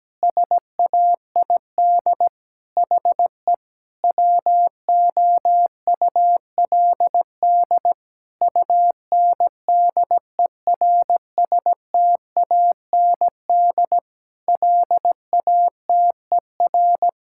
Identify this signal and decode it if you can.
Morse code, heard as a keyed tone, about 17 wpm, 700 Hz. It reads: SAID HE WOULD UNDERSTAND LATER
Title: Sentences from Top 1000 Words - Review Part 5 - 17wpm